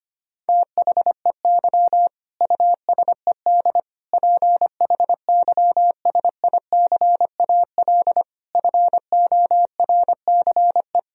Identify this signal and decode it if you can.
Morse code, heard as a keyed tone, about 25 wpm, 700 Hz. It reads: THEY USED PHYSICAL FORCE
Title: Sentences from Top 1000 Words - Review Part 4 - 25wpm